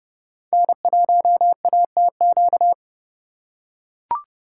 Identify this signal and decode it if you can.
Morse code, heard as a keyed tone, about 28 wpm, 700 Hz. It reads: N1ATQ